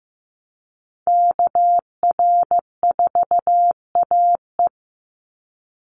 Morse code, heard as a keyed tone, about 15 wpm, 700 Hz. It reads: KR4AE
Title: Call Signs Repeated 2x - 15wpm